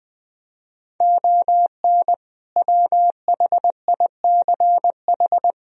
Morse code, heard as a keyed tone, about 20 wpm, 700 Hz. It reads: ON WHICH
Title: Sentences from Top 1000 Words - Review Part 1 - 20wpm